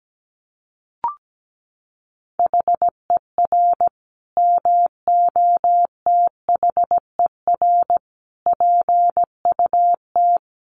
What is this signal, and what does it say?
Morse code, heard as a keyed tone, about 17 wpm, 700 Hz. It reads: HER MOTHER PUT
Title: Sentences from Top 800 Words 17wpm